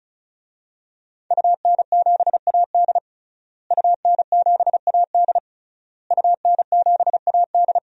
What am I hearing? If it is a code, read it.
Morse code, heard as a keyed tone, about 36 wpm, 700 Hz. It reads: UN7AD UN7AD UN7AD
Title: DXCC-100 Call Signs - 5 Characters - Repeated 3x - 35wpm